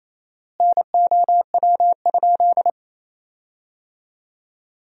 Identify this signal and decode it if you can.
Morse code, heard as a keyed tone, about 28 wpm, 700 Hz. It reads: NOW?
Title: 4x Word-Spacing - Sentences from Top 200 Words - 28wpm